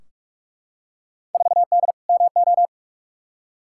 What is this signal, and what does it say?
Morse code, heard as a keyed tone, about 42 wpm, 700 Hz. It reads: 4D MO
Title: Field Day Exchanges Encore - 45wpm